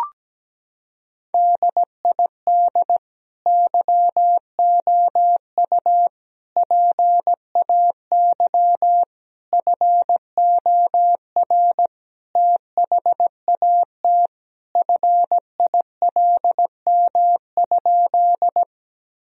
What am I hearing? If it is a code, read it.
Morse code, heard as a keyed tone, about 17 wpm, 700 Hz. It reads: DID YOU PAY FOR THAT FILM?
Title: Sentences from Top 1000 Words - Review Part 3 - 17wpm